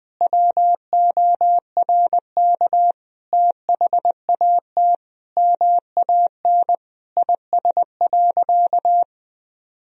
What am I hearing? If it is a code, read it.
Morse code, heard as a keyed tone, about 20 wpm, 700 Hz. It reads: WORK THAT MAN IS.
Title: Sentences from Top 1000 Words - Review Part 2 - 20wpm